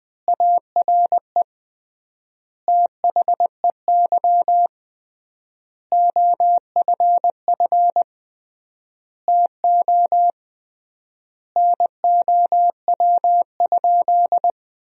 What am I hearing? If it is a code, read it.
Morse code, heard as a keyed tone, about 20 wpm, 700 Hz. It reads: ARE THEY OFF TO NOW?
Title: 3x Word-Spacing - Sentences from Top 200 Words - 20wpm